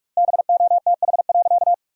Morse code, heard as a keyed tone, about 42 wpm, 700 Hz. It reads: BOTH.
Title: 4x Word-Spacing - Sentences from Top 300 Words - 45wpm